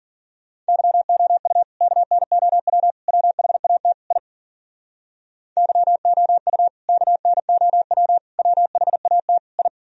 Morse code, heard as a keyed tone, about 52 wpm, 700 Hz. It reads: YOU KNOW WHAT I YOU KNOW WHAT I